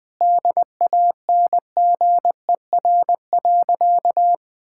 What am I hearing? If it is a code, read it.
Morse code, heard as a keyed tone, about 21 wpm, 690 Hz. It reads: DANGER.
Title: Sentences with 4-Word Phrases - Spoken Beginning - 20wpm